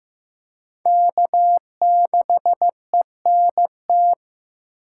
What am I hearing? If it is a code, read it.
Morse code, heard as a keyed tone, about 18 wpm, 700 Hz. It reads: K6ENT
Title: Speed Racing - USA 1x3 Call Signs 15wpm